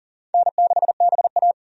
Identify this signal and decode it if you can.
Morse code, heard as a keyed tone, about 40 wpm, 700 Hz. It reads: N6BA
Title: USA Call Signs - 4 Characters - 40wpm